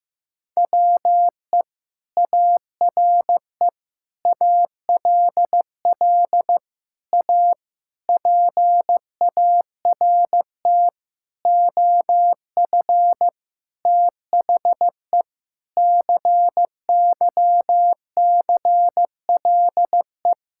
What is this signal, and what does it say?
Morse code, heard as a keyed tone, about 15 wpm, 700 Hz. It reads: WE ARE ALL A PART OF THE CYCLE
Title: Sentences from Top 1000 Words - Review Part 6 - 15wpm